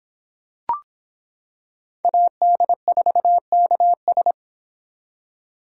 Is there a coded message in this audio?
Yes: Morse code from 2.0 to 4.3 s, heard as a keyed tone, about 25 wpm, 700 Hz. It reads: AD4KS